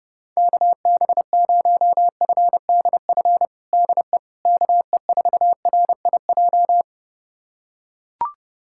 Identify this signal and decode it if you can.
Morse code, heard as a keyed tone, about 32 wpm, 700 Hz. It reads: KB0FDF DE KE4RIJ
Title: USA Callsign DE Callsign 30wpm